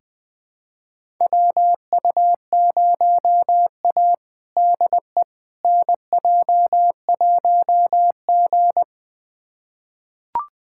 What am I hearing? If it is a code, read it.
Morse code, heard as a keyed tone, about 20 wpm, 700 Hz. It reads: WU0A DE NJ1G
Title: USA Callsign DE Callsign - 4 Characters - 20wpm